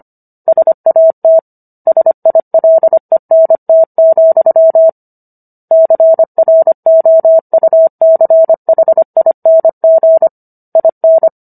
Morse code, heard as a keyed tone, about 25 wpm, 650 Hz. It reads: SAT SILENT, CROUCHING IN
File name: Book One / 16-Chapter 16 - 25 WPM